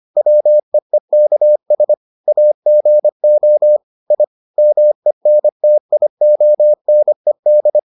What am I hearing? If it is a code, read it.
Morse code, heard as a keyed tone, about 25 wpm, 600 Hz. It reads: WEEKS AGO I MENTIONED